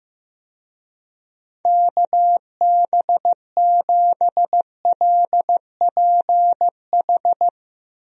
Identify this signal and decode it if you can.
Morse code, heard as a keyed tone, about 15 wpm, 700 Hz. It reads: KB7LPH
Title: USA Call Signs - 6 Characters - Repeated 3x - 15wpm